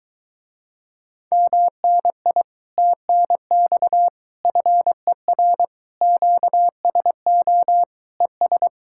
Morse code, heard as a keyed tone, about 22 wpm, 700 Hz. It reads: MNI TNX FER QSO ES